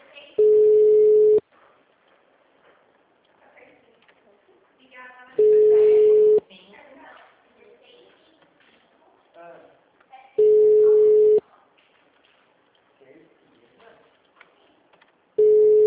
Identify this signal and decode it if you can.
Ringback tone (Europe, 425 Hz), with speech.